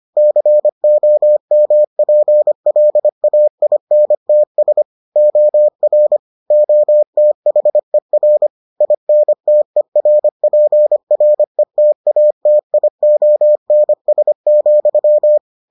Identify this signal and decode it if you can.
Morse code, heard as a keyed tone, about 25 wpm, 600 Hz. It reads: COMPLAINTS OR OTHER INTERPRETATIONS,